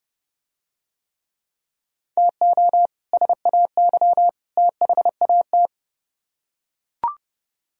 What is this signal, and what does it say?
Morse code, heard as a keyed tone, about 29 wpm, 700 Hz. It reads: TO SAY THAT